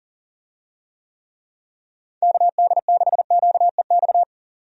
Morse code, heard as a keyed tone, about 40 wpm, 700 Hz. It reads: KD6QEX
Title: USA Call Signs - 6 Characters - 40wpm